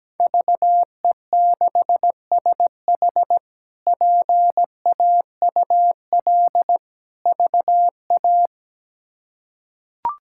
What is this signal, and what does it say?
Morse code, heard as a keyed tone, about 17 wpm, 700 Hz. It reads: VE6SH PAUL VA
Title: K1USN SST QSO Practice - 17wpm